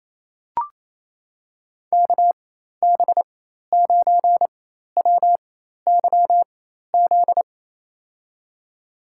Morse code, heard as a keyed tone, about 28 wpm, spaced as for 13 wpm, 700 Hz. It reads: KB9WYZ
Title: USA Call Signs - 6 Characters - 4x Character Spacing - 28wpm